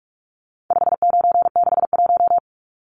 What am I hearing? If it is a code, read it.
Morse code, heard as a keyed tone, about 44 wpm, 700 Hz. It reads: 5961
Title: Numbers - 4-Digit 45wpm